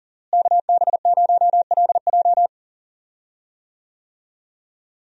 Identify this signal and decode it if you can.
Morse code, heard as a keyed tone, about 40 wpm, 700 Hz. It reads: KB0LJ
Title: USA Call Signs - 5 Characters - Repeated 2x - 40wpm